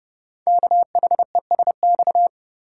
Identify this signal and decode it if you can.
Morse code, heard as a keyed tone, about 30 wpm, 700 Hz. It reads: KHESX
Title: Code Groups - 5 Letters - 30wpm